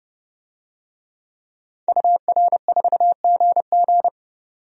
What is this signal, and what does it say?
Morse code, heard as a keyed tone, about 32 wpm, 700 Hz. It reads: UR4GG
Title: DXCC-100 Call Signs - 5 Characters - 30wpm